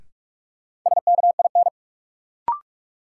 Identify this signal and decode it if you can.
Morse code, heard as a keyed tone, about 46 wpm, 700 Hz. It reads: SKIN
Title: Sentences from Top 1300 Words - New Word Drill - 45wpm